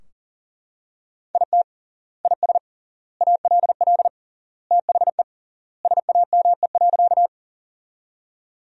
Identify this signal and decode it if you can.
Morse code, heard as a keyed tone, about 40 wpm, 700 Hz. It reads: IT IS ALL THE SAME.
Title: 3x Word-Spacing - Sentences from Top 200 Words - 40wpm